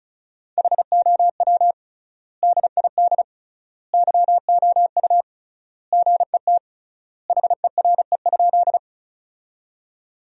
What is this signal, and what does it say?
Morse code, heard as a keyed tone, about 34 wpm, 700 Hz. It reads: HOW DID YOU GET HERE?
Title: 3x Word-Spacing - Sentences from Top 200 Words - 35wpm